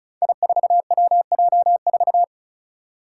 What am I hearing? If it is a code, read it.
Morse code, heard as a keyed tone, about 34 wpm, 700 Hz. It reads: I4WJ4
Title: Five Challenging Letter-Number Combinations 35wpm